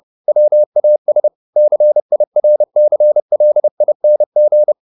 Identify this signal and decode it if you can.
Morse code, heard as a keyed tone, about 30 wpm, 600 Hz. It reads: WAS CIRCLING
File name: the old man and the sea / the old man and the sea 1 30wpm